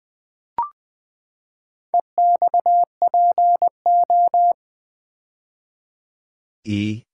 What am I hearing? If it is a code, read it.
Morse code, heard as a keyed tone, about 20 wpm, 700 Hz. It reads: EXPO
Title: Build A Word - 4 Letters At A Time - 8-Letter Words - 20wpm